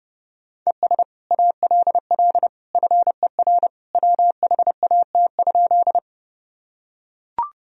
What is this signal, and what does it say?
Morse code, heard as a keyed tone, about 30 wpm, 700 Hz. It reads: ES ALL FER WHAT?
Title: Sentences from Top 100 Words and Abbreviations 30wpm